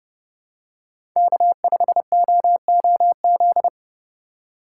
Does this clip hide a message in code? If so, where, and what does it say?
Yes: Morse code from 1.2 to 3.7 s, heard as a keyed tone, about 30 wpm, 700 Hz. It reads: K5OOZ